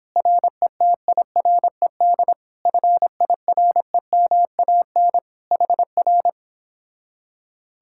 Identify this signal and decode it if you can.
Morse code, heard as a keyed tone, about 29 wpm, 700 Hz. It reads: RETIRED FIREMAN HR